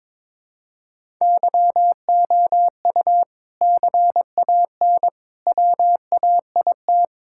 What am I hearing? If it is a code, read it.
Morse code, heard as a keyed tone, about 21 wpm, 700 Hz. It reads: YOU CAN WAIT
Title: Sentences from Top 1000 Words - Review Part 6 - 22wpm